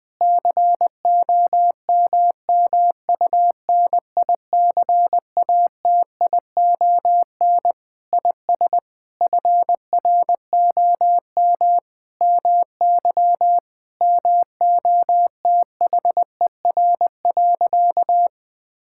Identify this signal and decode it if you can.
Morse code, heard as a keyed tone, about 20 wpm, 700 Hz. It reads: COMMUNICATION IS FROM MY MOTHER.
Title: Sentences from Top 1000 Words - Review Part 6 - 20wpm